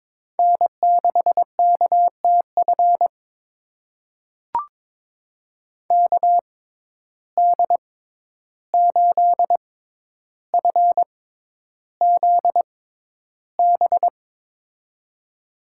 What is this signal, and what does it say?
Morse code, heard as a keyed tone, about 22 wpm, spaced as for 11 wpm, 700 Hz. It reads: N6KTF KD8FZB
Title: USA Call Signs - 6x Character Spacing - 22wpm